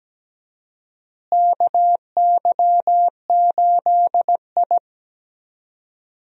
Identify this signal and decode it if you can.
Morse code, heard as a keyed tone, about 17 wpm, 700 Hz. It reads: KY8I
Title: USA Call Signs - 4 Characters - Repeated 3x - 17wpm